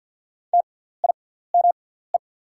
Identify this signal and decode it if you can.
Morse code, heard as a keyed tone, about 48 wpm, spaced as for 17 wpm, 690 Hz. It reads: TIME
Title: 6x Character Spacing - Top 500 Words - 50wpm